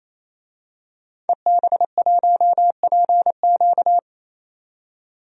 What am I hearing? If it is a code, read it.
Morse code, heard as a keyed tone, about 28 wpm, 700 Hz. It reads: EB1PQ
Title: DXCC-100 Call Signs - 5 Characters - 28wpm